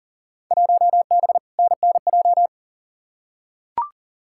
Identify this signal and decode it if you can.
Morse code, heard as a keyed tone, about 40 wpm, 700 Hz. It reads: 1B NNJ
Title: Field Day Exchanges Encore - 40wpm